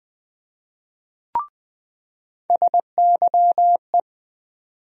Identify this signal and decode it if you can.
Morse code, heard as a keyed tone, about 21 wpm, 700 Hz. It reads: SYE
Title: USA Call Sign Postfixes - 3 Letters 20wpm